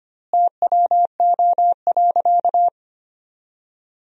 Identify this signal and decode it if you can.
Morse code, heard as a keyed tone, about 25 wpm, 700 Hz. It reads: TWO.
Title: 2x Word-Spacing - Sentences from Top 200 Words - 25wpm